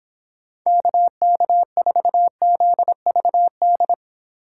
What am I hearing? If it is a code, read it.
Morse code, heard as a keyed tone, about 26 wpm, 700 Hz. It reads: KK4ZVD